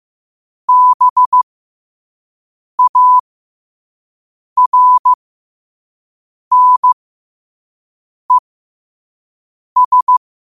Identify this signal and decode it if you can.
Morse code, heard as a keyed tone, about 15 wpm, spaced as for 5 wpm, 1000 Hz. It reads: BARNES
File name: qotdfiles / 5wpm